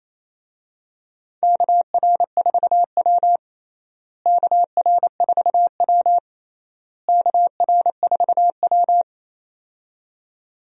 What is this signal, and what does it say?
Morse code, heard as a keyed tone, about 28 wpm, 700 Hz. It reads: KR4W KR4W KR4W
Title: USA Call Signs - 4 Characters - Repeated 3x - 28wpm